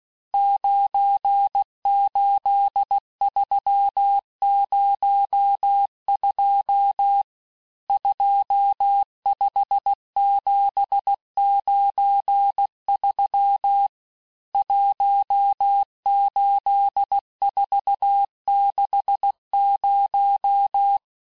Morse code, heard as a keyed tone, about 16 wpm, 800 Hz. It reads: 98302 25793 18460
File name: cours-du-lundi files / cours-1592--teste-chiffre-vitesse-15mpm